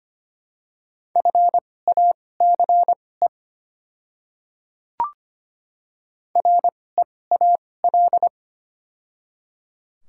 Morse code, heard as a keyed tone, about 25 wpm, spaced as for 18 wpm, 700 Hz. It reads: FACE REAL
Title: 2x Character Spacing - Top 300 Words - 25wpm